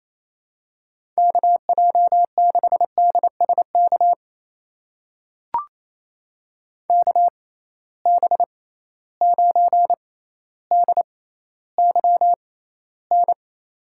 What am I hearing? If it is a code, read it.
Morse code, heard as a keyed tone, about 28 wpm, spaced as for 14 wpm, 700 Hz. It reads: KJ6DSK KB9DYN
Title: USA Call Signs - 6 Characters - 6x Character Spacing - 28wpm